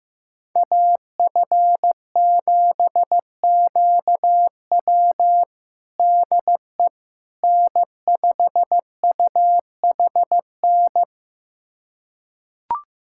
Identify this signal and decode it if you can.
Morse code, heard as a keyed tone, about 15 wpm, 700 Hz. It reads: AF7QW DE N5UHN
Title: USA Callsign DE Callsign - 5 Characters - 15wpm